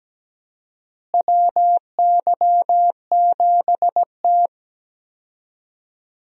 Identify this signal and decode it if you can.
Morse code, heard as a keyed tone, about 17 wpm, 700 Hz. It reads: WY7T
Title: USA Call Signs - 4 Characters - Repeated 2x - 17wpm